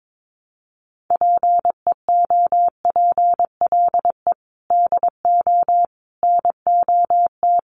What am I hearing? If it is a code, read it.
Morse code, heard as a keyed tone, about 22 wpm, 700 Hz. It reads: PEOPLE DO NOT
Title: Sentences from Top 1200 Words 22wpm